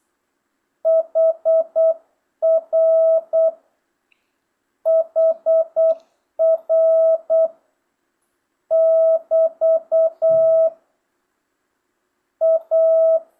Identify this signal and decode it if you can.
Morse code, heard as a keyed tone, about 8 wpm, 650 Hz. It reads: HR HR <BT> A
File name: jh8cbh / 2025cw